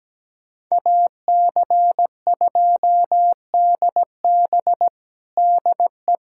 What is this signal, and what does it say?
Morse code, heard as a keyed tone, about 17 wpm, 700 Hz. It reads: AC2DB DE
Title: USA Callsign DE Callsign - 5 Characters - 17wpm